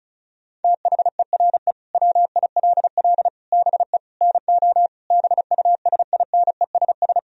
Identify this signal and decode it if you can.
Morse code, heard as a keyed tone, about 36 wpm, 700 Hz. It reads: THERE WILL BE NO BUSINESS
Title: Sentences from Top 1000 Words - Review Part 4 - 35wpm